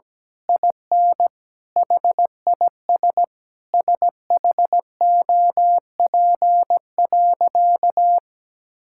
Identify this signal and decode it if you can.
Morse code, heard as a keyed tone, about 17 wpm, 700 Hz. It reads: IN HIS SHOP.
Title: Sentences from Top 1600 Words 17wpm